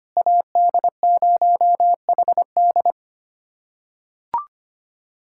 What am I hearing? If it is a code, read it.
Morse code, heard as a keyed tone, about 25 wpm, 700 Hz. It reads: AD0HD
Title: SOTA QSO Practice 25wpm